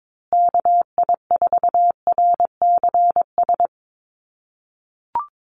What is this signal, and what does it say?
Morse code, heard as a keyed tone, about 22 wpm, 700 Hz. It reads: KI4RCS
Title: USA Call Signs Repeated 2x 22wpm